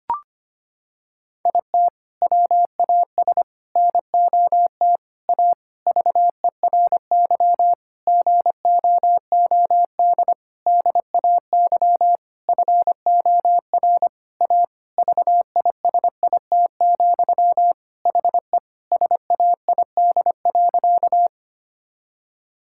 Morse code, heard as a keyed tone, about 25 wpm, 700 Hz. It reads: IT WAS NOT A VERY GOOD DAY FOR A VISIT, HE SAID.
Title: Sentences from Top 1000 Words - Review Part 6 - 25wpm